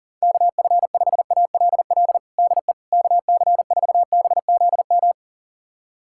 Morse code, heard as a keyed tone, about 37 wpm, 700 Hz. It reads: KF5ALL DE KC4BZM
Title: USA Callsign DE Callsign - 6 Characters - 40wpm